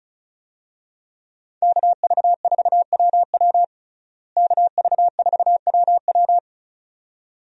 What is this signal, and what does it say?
Morse code, heard as a keyed tone, about 36 wpm, 700 Hz. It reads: KV4WW KV4WW
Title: USA Call Signs - 5 Characters - Repeated 2x - 35wpm